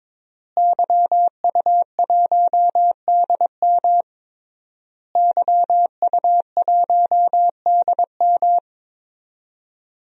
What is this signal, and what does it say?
Morse code, heard as a keyed tone, about 22 wpm, 700 Hz. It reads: YU1DM YU1DM
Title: DXCC-100 Call Signs - 5 Characters - Repeated 3x - 22wpm